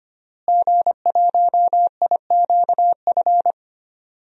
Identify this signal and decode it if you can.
Morse code, heard as a keyed tone, about 25 wpm, 700 Hz. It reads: G1IQF
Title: DXCC-100 Call Signs - 5 Characters - Repeated 2x - 25wpm